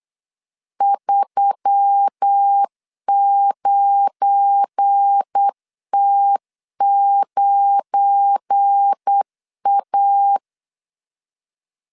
Morse code, heard as a keyed tone, about 8 wpm, 800 Hz. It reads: 39T9A
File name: LEZIONI / 7 GF9Z / 7 GF9 8wpm 3545